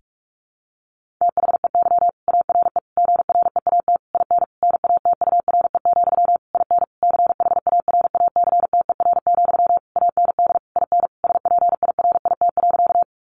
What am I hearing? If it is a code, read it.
Morse code, heard as a keyed tone, about 44 wpm, 700 Hz. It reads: THEY ARE GREAT IN NATURE, IN CHARACTER, AND IN SPIRIT.
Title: Sentences from Top 1400 Words 45wpm